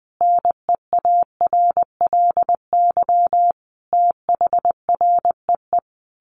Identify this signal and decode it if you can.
Morse code, heard as a keyed tone, about 20 wpm, 700 Hz. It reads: NEARLY THREE
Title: Sentences from Top 1300 Words 20wpm